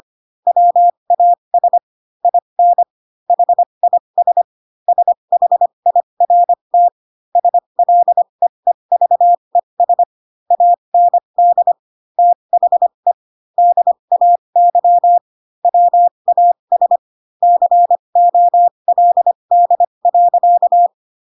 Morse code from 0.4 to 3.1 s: WAS IN